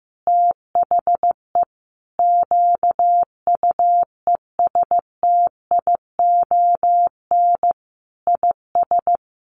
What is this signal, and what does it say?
Morse code, heard as a keyed tone, about 14 wpm, 700 Hz. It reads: THE QUESTION IS
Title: Sentences from Top 1200 Words 15wpm